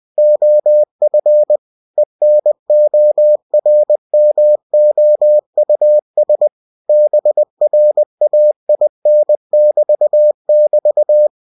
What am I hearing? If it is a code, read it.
Morse code, heard as a keyed tone, about 20 wpm, 600 Hz. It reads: OF ENORMOUS BRAIN<BT><BT>
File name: cw-books / winnie-the-pooh / winnie-20wpm-page-0104